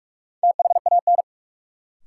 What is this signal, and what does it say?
Morse code, heard as a keyed tone, about 43 wpm, 700 Hz. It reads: THAN
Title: Sentences from Top 100 Words - New Word Drill - 45wpm